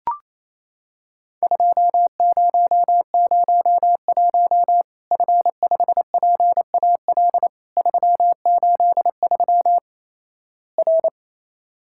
Morse code, heard as a keyed tone, about 27 wpm, 700 Hz. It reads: 2001 F5PAL 383 R
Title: Worked All Europe DX Contest - QTC Exchange - 28wpm